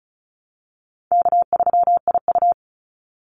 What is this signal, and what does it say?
Morse code, heard as a keyed tone, about 34 wpm, 700 Hz. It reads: K3IU
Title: Easy Call Signs Repeated 3x - 35wpm